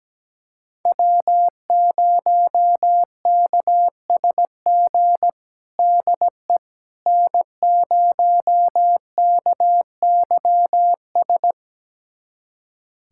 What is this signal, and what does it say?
Morse code, heard as a keyed tone, about 17 wpm, 700 Hz. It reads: W0KSG DE N0KYS
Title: USA Callsign DE Callsign - 5 Characters - 17wpm